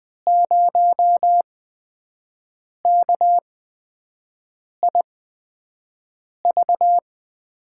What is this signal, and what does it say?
Morse code, heard as a keyed tone, about 20 wpm, spaced as for 5 wpm, 700 Hz. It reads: 0KIV
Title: USA Call Signs - 5 Characters - 8x Character Spacing - 20wpm